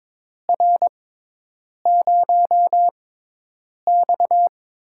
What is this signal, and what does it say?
Morse code, heard as a keyed tone, about 22 wpm, spaced as for 8 wpm, 700 Hz. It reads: R0X
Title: USA Call Signs - 6x Character Spacing - 22wpm